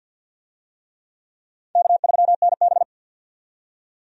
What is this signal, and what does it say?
Morse code, heard as a keyed tone, about 51 wpm, 700 Hz. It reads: K3NB
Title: USA Call Signs - 4 Characters - 50wpm